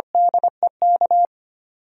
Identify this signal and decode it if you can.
Morse code, heard as a keyed tone, about 25 wpm, 700 Hz. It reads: DEK
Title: DXCC-100 Call Signs - 6 Characters - Repeated 3x - 25wpm